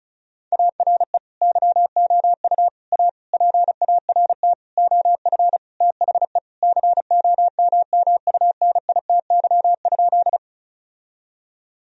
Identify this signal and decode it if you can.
Morse code, heard as a keyed tone, about 36 wpm, 700 Hz. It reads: ARE YOU A PART OF THE COMMUNITY?
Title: Sentences from Top 1000 Words - Review Part 4 - 35wpm